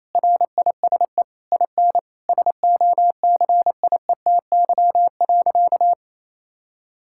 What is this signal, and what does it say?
Morse code, heard as a keyed tone, about 28 wpm, 700 Hz. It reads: RISE IN SOCIETY.
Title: Sentences from Top 1000 Words - Review Part 6 - 28wpm